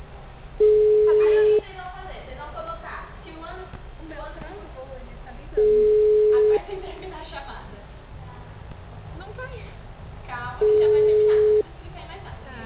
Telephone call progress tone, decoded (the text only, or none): ringback tone